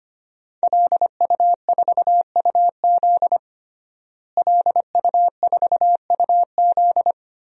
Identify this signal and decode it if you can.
Morse code, heard as a keyed tone, about 25 wpm, 700 Hz. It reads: LU4UZ LU4UZ
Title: DXCC-100 Call Signs - 5 Characters - Repeated 2x - 25wpm